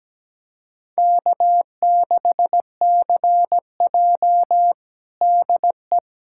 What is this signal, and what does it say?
Morse code, heard as a keyed tone, about 17 wpm, 700 Hz. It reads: K6CJ DE
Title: USA Callsign DE Callsign - 4 Characters - 17wpm